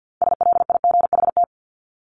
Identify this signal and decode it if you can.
Morse code, heard as a keyed tone, about 50 wpm, 700 Hz. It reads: SLIGHT
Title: LY Suffix - Top 100 Words with Base Words 50wpm